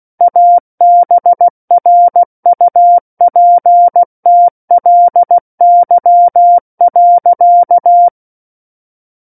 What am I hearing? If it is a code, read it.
Morse code, heard as a keyed tone, about 16 wpm, 700 Hz. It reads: ABRUPTLY.
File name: Book One / 07-Chapter 7 - 16 WPM